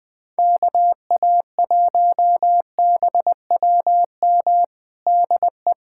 Morse code, heard as a keyed tone, about 20 wpm, 700 Hz. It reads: KA1BWM DE
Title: USA Callsign DE Callsign 20wpm